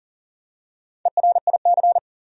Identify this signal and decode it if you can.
Morse code, heard as a keyed tone, about 37 wpm, 700 Hz. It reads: ERIC
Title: USA Common Names - Max 4 Letters - 40wpm